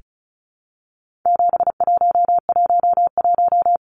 Morse code, heard as a keyed tone, about 34 wpm, 700 Hz. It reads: 7111
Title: Numbers - 4-Digit 35wpm